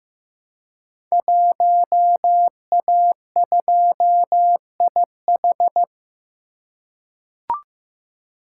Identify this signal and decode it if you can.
Morse code, heard as a keyed tone, about 15 wpm, 700 Hz. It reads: 1A2IH
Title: Code Groups - 5 Letter:Numbers - 15wpm